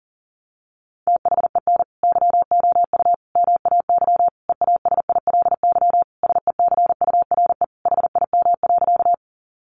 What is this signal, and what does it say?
Morse code, heard as a keyed tone, about 40 wpm, 700 Hz. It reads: THEN YOU MAY EASILY SECURE HIM.
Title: Sentences from Top 1200 Words 40wpm